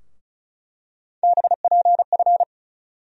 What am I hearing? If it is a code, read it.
Morse code, heard as a keyed tone, about 34 wpm, 700 Hz. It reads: BPF
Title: USA Call Sign Postfixes - 3 Letters 35wpm